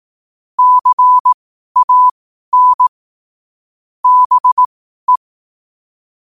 Morse code, heard as a keyed tone, about 18 wpm, spaced as for 12 wpm, 1000 Hz. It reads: CAN BE
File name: qotdfiles / 12wpm